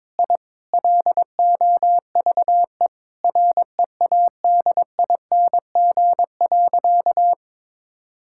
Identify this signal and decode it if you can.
Morse code, heard as a keyed tone, about 22 wpm, 700 Hz. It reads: I LOVE READING.
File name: Sentences with 7-Word Phrases - 22wpm